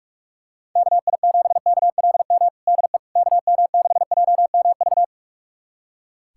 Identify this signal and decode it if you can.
Morse code, heard as a keyed tone, about 43 wpm, 700 Hz. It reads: KI7KLM DE KM6JMV
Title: USA Callsign DE Callsign - 6 Characters - 45wpm